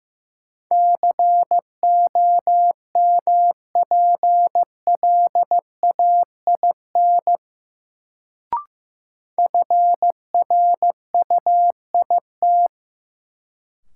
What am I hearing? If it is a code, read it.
Morse code, heard as a keyed tone, about 15 wpm, 700 Hz. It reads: COMPLAIN FRUIT
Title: Sentences from Top 1300 Words - New Word Drill - 15wpm